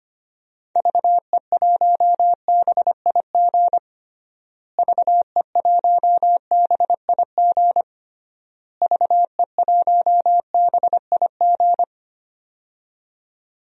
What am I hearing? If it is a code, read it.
Morse code, heard as a keyed tone, about 25 wpm, 700 Hz. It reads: VE1BIG VE1BIG VE1BIG
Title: DXCC-100 Call Signs - 6 Characters - Repeated 3x - 25wpm